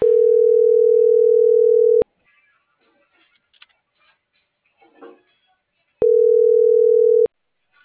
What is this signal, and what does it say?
Ringback tone (North America), 440 + 480 Hz.